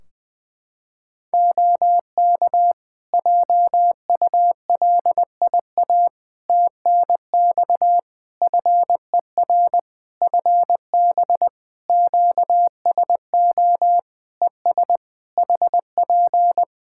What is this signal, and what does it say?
Morse code, heard as a keyed tone, about 20 wpm, 700 Hz. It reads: OK JULIA TNX FER FB QSO ES HP
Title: Practice QSO - Exchange 4 - Ending - 20wpm